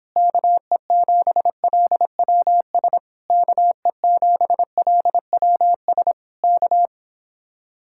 Morse code, heard as a keyed tone, about 26 wpm, 700 Hz. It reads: KE7LWS KE7LWS K